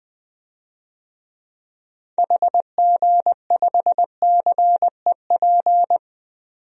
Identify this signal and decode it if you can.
Morse code, heard as a keyed tone, about 20 wpm, 700 Hz. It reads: HG5CEP
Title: DXCC-100 Call Signs - 6 Characters - 20wpm